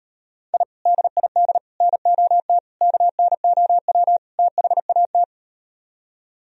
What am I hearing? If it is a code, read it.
Morse code, heard as a keyed tone, about 38 wpm, 700 Hz. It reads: I DID NOT KNOW THAT